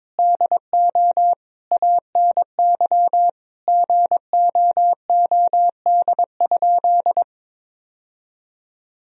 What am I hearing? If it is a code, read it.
Morse code, heard as a keyed tone, about 21 wpm, 700 Hz. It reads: DO ANY GOOD?
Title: Sentences from Top 1000 Words - Review Part 4 - 22wpm